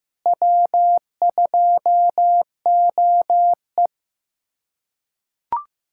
Morse code, heard as a keyed tone, about 15 wpm, 700 Hz. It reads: W2OE
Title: USA Callsign DE Callsign - 4 Characters - 15wpm